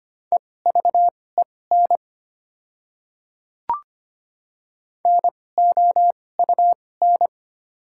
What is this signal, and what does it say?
Morse code, heard as a keyed tone, about 25 wpm, spaced as for 18 wpm, 700 Hz. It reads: EVEN NOUN
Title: 2x Character Spacing - Top 500 Words - 25wpm